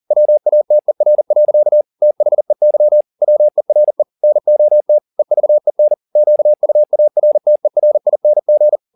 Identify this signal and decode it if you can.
Morse code, heard as a keyed tone, about 40 wpm, 600 Hz. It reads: WATER. THEY WERE NOT EVEN QUARTERING
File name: the old man and the sea 2 40wpm